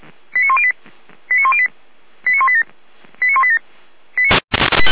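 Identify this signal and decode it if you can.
Five-tone selective call (ZVEI), heard as quick runs of five tones, tones of about 70 ms. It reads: 89108 89108 89187 89187
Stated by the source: ZVEI five-tone selective call 89067 received with an SDR, from the Feuerwehr Orlamünde audio server.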